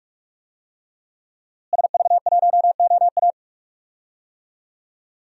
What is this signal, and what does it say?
Morse code, heard as a keyed tone, about 46 wpm, 700 Hz. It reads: SV1OA
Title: DXCC-100 Call Signs - 5 Characters - Repeated 3x - 45wpm